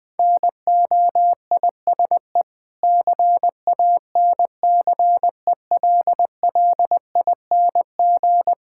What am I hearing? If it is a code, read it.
Morse code, heard as a keyed tone, about 20 wpm, 690 Hz. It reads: NOISE CANCELLING
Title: Difficult Words with Hints - 20wpm